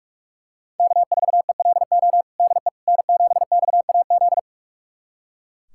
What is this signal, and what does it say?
Morse code, heard as a keyed tone, about 41 wpm, 700 Hz. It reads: K4ELO DE N7XAZ
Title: USA Callsign DE Callsign 45wpm